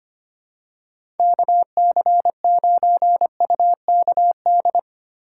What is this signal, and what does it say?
Morse code, heard as a keyed tone, about 25 wpm, 700 Hz. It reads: KC9UKD
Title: USA Call Signs - 6 Characters - 2x Character Spacing - 25wpm